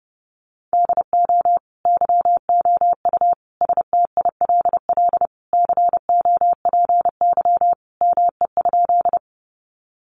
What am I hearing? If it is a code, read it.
Morse code, heard as a keyed tone, about 30 wpm, 700 Hz. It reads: DO YOU STILL COPY ME?
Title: Sentences from Top 1400 Words 30wpm